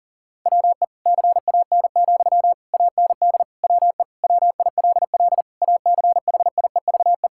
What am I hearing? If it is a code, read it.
Morse code, heard as a keyed tone, about 40 wpm, 690 Hz. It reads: WE CAN, AND WE WILL ACHIEVE
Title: Sentences from Top 1000 Words - Review Part 6 - 40wpm